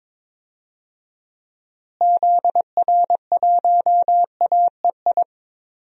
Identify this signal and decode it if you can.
Morse code, heard as a keyed tone, about 22 wpm, 700 Hz. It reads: ZR1AEI
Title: DXCC-100 Call Signs - 6 Characters - Repeated 3x - 22wpm